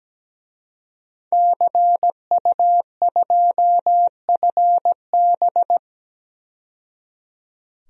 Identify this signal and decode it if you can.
Morse code, heard as a keyed tone, about 17 wpm, 700 Hz. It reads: CU2FB
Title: DXCC-100 Call Signs - 5 Characters - 17wpm